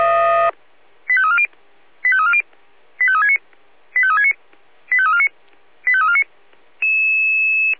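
Five-tone selective call (ZVEI), heard as quick runs of five tones, tones of about 70 ms. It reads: 85340 85340 85369 85369 85349 85349
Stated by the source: ZVEI five-tone selective call received by SDR.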